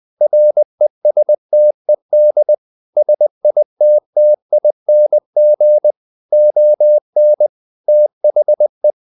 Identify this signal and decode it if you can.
Morse code, heard as a keyed tone, about 20 wpm, 600 Hz. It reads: RESTED SITTING ON THE
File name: the old man and the sea / the old man and the sea part 1 20wpm